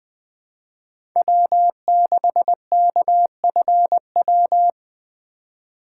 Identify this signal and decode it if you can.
Morse code, heard as a keyed tone, about 20 wpm, 700 Hz. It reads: W6KFW
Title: Speed Racing - USA 1x3 Call Signs 17wpm